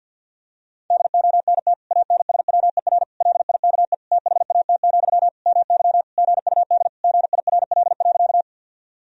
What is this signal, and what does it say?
Morse code, heard as a keyed tone, about 51 wpm, 700 Hz. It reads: DONT ANSWER LIKE THAT, MY GUD GIRL.
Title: Sentences from Top 400 Words and Abbreviations 50wpm